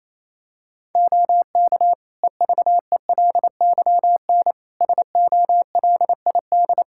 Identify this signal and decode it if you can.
Morse code, heard as a keyed tone, about 28 wpm, 700 Hz. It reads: OK EVELYN SOLID
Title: Practice QSO - Exchange 3 - More Info - 28wpm